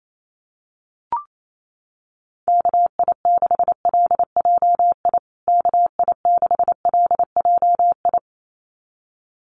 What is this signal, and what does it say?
Morse code, heard as a keyed tone, about 28 wpm, 700 Hz. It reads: KI6LJI KI6LJI
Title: USA Call Signs Repeated 2x 28wpm